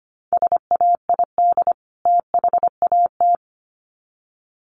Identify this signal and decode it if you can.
Morse code, heard as a keyed tone, about 25 wpm, 700 Hz. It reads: SAID THAT
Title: Sets of 2 Words 25wpm